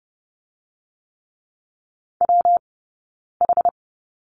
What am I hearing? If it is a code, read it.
Morse code, heard as a keyed tone, about 30 wpm, 700 Hz. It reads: W H
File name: Lesson 19d - Letter w - Two Characters 30wpm